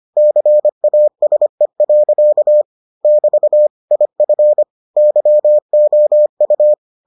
Morse code, heard as a keyed tone, about 25 wpm, 600 Hz. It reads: CASE. <BT> IF YOU